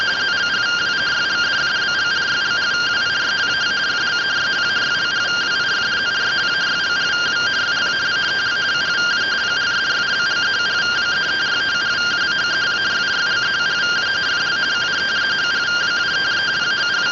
Two-tone teleprinter signal, about 170 Hz shift.